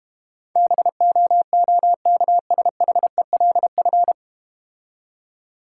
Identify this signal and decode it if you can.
Morse code, heard as a keyed tone, about 32 wpm, 700 Hz. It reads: BOOKSHELF